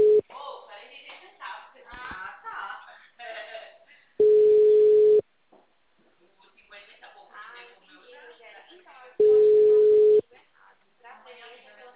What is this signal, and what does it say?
Ringback tone (Europe, 425 Hz), with speech.
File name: out- -2007- - - .24226